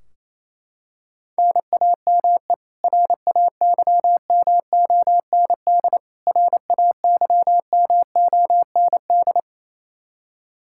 Morse code, heard as a keyed tone, about 28 wpm, 700 Hz. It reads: NAME RAYMOND RAYMOND
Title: Practice QSO - Exchange 1 - Beginning - 28wpm